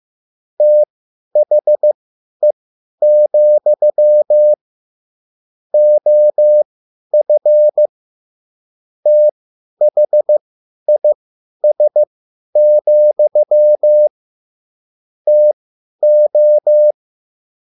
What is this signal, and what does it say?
Morse code, heard as a keyed tone, about 15 wpm, spaced as for 10 wpm, 600 Hz. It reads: THE, OF THIS, TO
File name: QSL / CW lesson